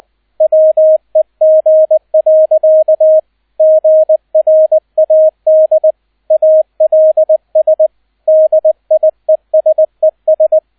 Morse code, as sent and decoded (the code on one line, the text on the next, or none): .-- . --. .-.-.- / --. .-. .- -.. / .- .-.. ... / -.. .. ... ...
WEG. GRAD ALS DIESES